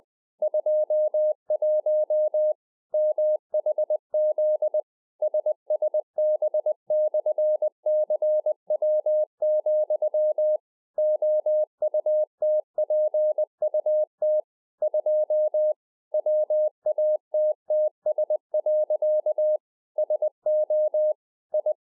Morse code, heard as a keyed tone, about 20 wpm, 610 Hz. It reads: 21 MHZ SSB/CW, OUTPUT 2 WATTS. SO I
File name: 2011-04-mp3 / A SMALL TRANCEIVER 3 20WPM